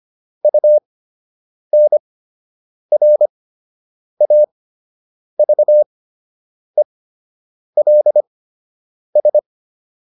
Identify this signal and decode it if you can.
Morse code, heard as a keyed tone, about 25 wpm, spaced as for 8 wpm, 600 Hz. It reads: UNRAVELS